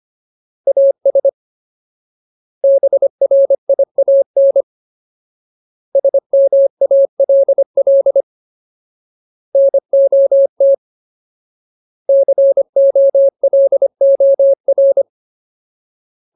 Morse code, heard as a keyed tone, about 25 wpm, 550 Hz. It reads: AS BRIAN SMALL NOT COLOR
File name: Ultimate Word List / 25WPM Ultimate